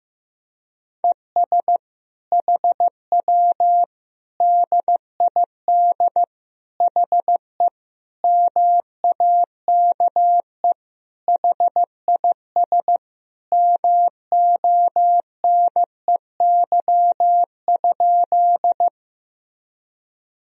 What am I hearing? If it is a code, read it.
Morse code, heard as a keyed tone, about 15 wpm, 700 Hz. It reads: ES HW DID HE MAKE HIS MONEY?
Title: Sentences from Top 400 Words and Abbreviations 15wpm